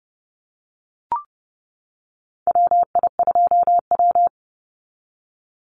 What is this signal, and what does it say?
Morse code, heard as a keyed tone, about 32 wpm, 700 Hz. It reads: WI2W
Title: NA SOTA Chaser Callsigns 30wpm